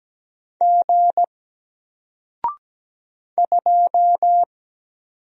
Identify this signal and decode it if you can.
Morse code, heard as a keyed tone, about 17 wpm, 700 Hz. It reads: G 2
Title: Single Character (Letter:Number:Punctuation:Prosign) 17wpm